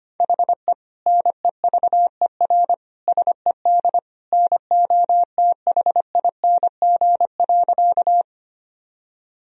Morse code, heard as a keyed tone, about 25 wpm, 700 Hz. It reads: HE NEVER SED NOTHING.
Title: Sentences from Top 500 Words and Abbreviations 25wpm